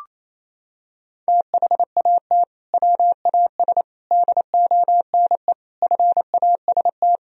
Morse code, heard as a keyed tone, about 28 wpm, 700 Hz. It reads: THAT WAS DONE FAST
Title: Sentences from Top 1000 Words - Review Part 2 - 28wpm